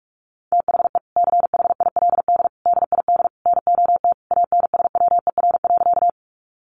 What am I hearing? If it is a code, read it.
Morse code, heard as a keyed tone, about 44 wpm, 700 Hz. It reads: THE CHILD DID NOT ANSWER.
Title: Sentences from Top 800 Words 45wpm